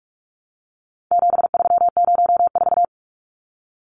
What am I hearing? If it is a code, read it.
Morse code, heard as a keyed tone, about 44 wpm, 700 Hz. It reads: 7304
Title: Numbers - 4-Digit 45wpm